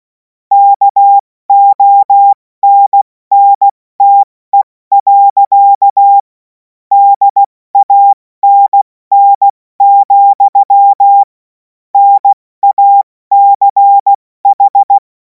Morse code, heard as a keyed tone, about 16 wpm, 800 Hz. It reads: KONNTE. DANN, NACH
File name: Morsekurs / Witze / Witz 11. August 2026 70 F80